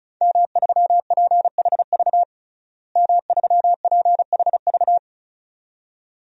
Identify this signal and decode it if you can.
Morse code, heard as a keyed tone, about 36 wpm, 700 Hz. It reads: M3PHV M3PHV
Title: DXCC-100 Call Signs - 5 Characters - Repeated 2x - 35wpm